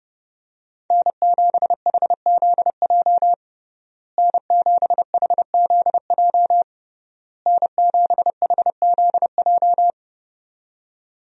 Morse code, heard as a keyed tone, about 30 wpm, 700 Hz. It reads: N7HZJ N7HZJ N7HZJ
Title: USA Call Signs - 5 Characters - Repeated 3x - 30wpm